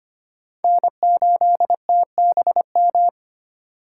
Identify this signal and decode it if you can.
Morse code, heard as a keyed tone, about 34 wpm, spaced as for 30 wpm, 700 Hz. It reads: N8TBM
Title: Speed Racing - USA 1x3 Call Signs 25wpm